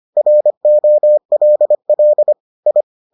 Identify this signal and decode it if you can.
Morse code, heard as a keyed tone, about 25 wpm, 600 Hz. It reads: ROLL I